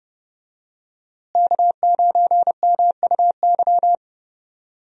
Morse code, heard as a keyed tone, about 30 wpm, 700 Hz. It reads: K9MUY
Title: USA Call Signs - 5 Characters - Repeated 3x - 30wpm